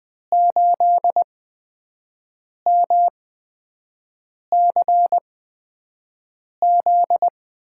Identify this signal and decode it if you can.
Morse code, heard as a keyed tone, about 20 wpm, spaced as for 5 wpm, 700 Hz. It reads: 8MCZ
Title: USA Call Signs - 6 Characters - 8x Character Spacing - 20wpm